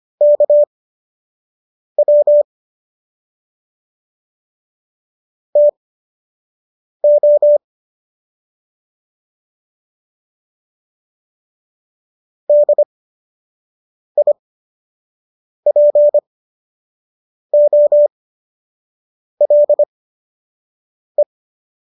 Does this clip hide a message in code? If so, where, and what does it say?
Yes: Morse code from 0.2 to 21.2 s, heard as a keyed tone, about 25 wpm, spaced as for 6 wpm, 590 Hz. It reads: KW TO DIPOLE